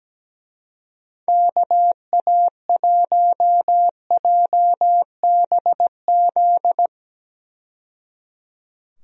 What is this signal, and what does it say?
Morse code, heard as a keyed tone, about 17 wpm, 700 Hz. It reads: KA1JBZ
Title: USA Call Signs - 6 Characters - 17wpm